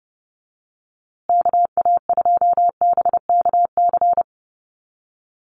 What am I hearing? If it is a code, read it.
Morse code, heard as a keyed tone, about 30 wpm, 700 Hz. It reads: KA2BKC
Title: USA Call Signs Repeated 3x 30wpm